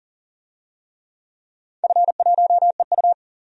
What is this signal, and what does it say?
Morse code, heard as a keyed tone, about 39 wpm, 700 Hz. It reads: F1EU
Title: DXCC-100 Call Signs - 4 Characters - 40wpm